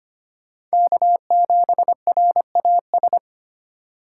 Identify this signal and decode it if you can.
Morse code, heard as a keyed tone, about 25 wpm, 700 Hz. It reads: K7RAS
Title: USA Call Signs - 5 Characters - 3x Character Spacing - 25wpm